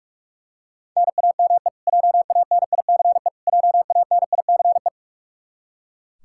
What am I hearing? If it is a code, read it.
Morse code, heard as a keyed tone, about 43 wpm, 700 Hz. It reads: NAME JANICE JANICE
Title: Practice QSO - Exchange 1 - Beginning - 45wpm